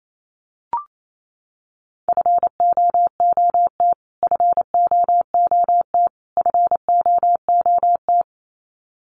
Morse code, heard as a keyed tone, about 28 wpm, 700 Hz. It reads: FOOT FOOT FOOT
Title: Top 500 Words Repeated 28wpm